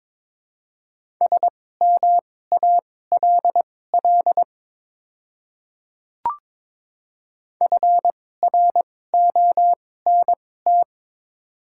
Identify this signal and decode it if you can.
Morse code, heard as a keyed tone, about 22 wpm, spaced as for 16 wpm, 700 Hz. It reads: SMALL FRONT
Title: 2x Character Spacing - Top 500 Words - 22wpm